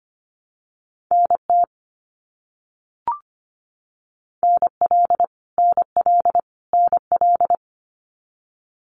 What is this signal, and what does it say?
Morse code, heard as a keyed tone, about 25 wpm, 700 Hz. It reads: NT NL NL NL
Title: Canadian Provinces and Territories - Repeated 3x - 25wpm